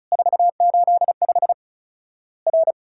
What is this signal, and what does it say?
Morse code, heard as a keyed tone, about 34 wpm, 700 Hz. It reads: 485 R
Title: Worked All Europe DX Contest - QTC Exchange - 35wpm